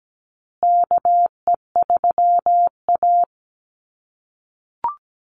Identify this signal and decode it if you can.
Morse code, heard as a keyed tone, about 17 wpm, 700 Hz. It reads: KE3A
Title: USA Call Signs Repeated 2x 17wpm